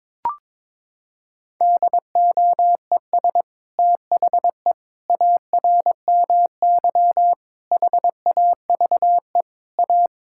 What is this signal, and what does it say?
Morse code, heard as a keyed tone, about 22 wpm, 700 Hz. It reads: DOES THE ARMY HAVE A
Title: Sentences from Top 1000 Words - Review Part 5 - 22wpm